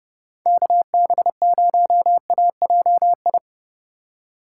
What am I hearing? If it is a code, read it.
Morse code, heard as a keyed tone, about 30 wpm, 700 Hz. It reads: KB0AJI
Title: USA Call Signs - 6 Characters - 3x Character Spacing - 30wpm